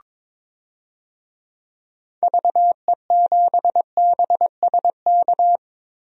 Morse code, heard as a keyed tone, about 22 wpm, 700 Hz. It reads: VE7BSK
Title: DXCC-100 Call Signs - 6 Characters - 22wpm